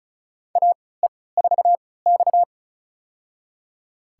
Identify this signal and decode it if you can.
Morse code, heard as a keyed tone, about 36 wpm, spaced as for 20 wpm, 700 Hz. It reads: AE4X
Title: USA Call Signs - 4 Characters - 3x Character Spacing - 35wpm